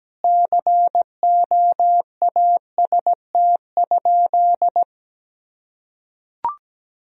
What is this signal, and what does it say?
Morse code, heard as a keyed tone, about 17 wpm, 700 Hz. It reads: COAST?
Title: Sentences from Top 1000 Words - Review Part 4 - 17wpm